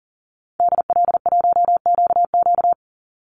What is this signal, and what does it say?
Morse code, heard as a keyed tone, about 40 wpm, 700 Hz. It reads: DL1QQ
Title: Call Signs - 40wpm